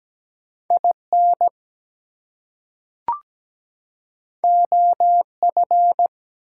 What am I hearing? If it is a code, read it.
Morse code, heard as a keyed tone, about 17 wpm, 700 Hz. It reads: IN OF
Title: Two Letter Words - 17wpm